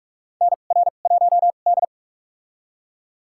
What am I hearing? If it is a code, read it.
Morse code, heard as a keyed tone, about 42 wpm, spaced as for 31 wpm, 700 Hz. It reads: NR1D
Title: USA Call Signs - 4 Characters - 2x Character Spacing - 45wpm